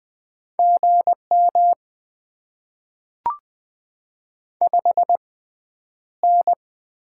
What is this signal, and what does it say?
Morse code, heard as a keyed tone, about 20 wpm, spaced as for 10 wpm, 690 Hz. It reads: GM 5N
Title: Top 20 QSO Elements - 6x Character Spacing - 20wpm